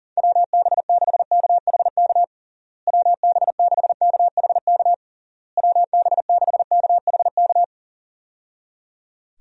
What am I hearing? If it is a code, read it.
Morse code, heard as a keyed tone, about 40 wpm, 700 Hz. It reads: WB6KHK WB6KHK WB6KHK
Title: USA Call Signs - 6 Characters - Repeated 3x - 40wpm